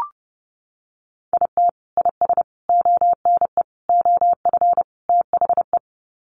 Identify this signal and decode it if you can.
Morse code, heard as a keyed tone, about 30 wpm, 700 Hz. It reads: IT IS ONE OF THE
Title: Sets of 5 Words 30wpm